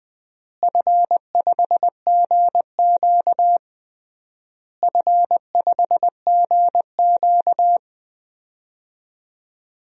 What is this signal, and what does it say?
Morse code, heard as a keyed tone, about 20 wpm, 700 Hz. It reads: F5GQ F5GQ
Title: DXCC-100 Call Signs - 4 Characters - Repeated 3x - 20wpm